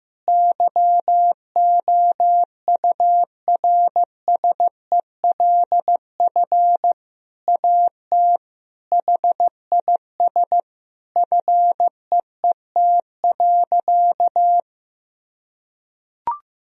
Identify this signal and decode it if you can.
Morse code, heard as a keyed tone, about 16 wpm, 700 Hz. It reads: YOURSELF AT HIS FEET.
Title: Sentences from Top 1000 Words - Review Part 1 - 15wpm